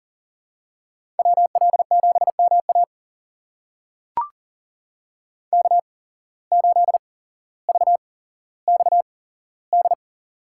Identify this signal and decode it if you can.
Morse code, heard as a keyed tone, about 40 wpm, spaced as for 17 wpm, 700 Hz. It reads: WL7MA K8VXD
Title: USA Call Signs - 5 Characters - 8x Character Spacing - 40wpm